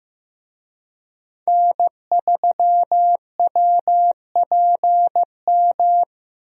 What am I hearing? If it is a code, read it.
Morse code, heard as a keyed tone, about 15 wpm, 700 Hz. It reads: N3WPM
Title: USA Call Signs - 5 Characters - Repeated 3x - 15wpm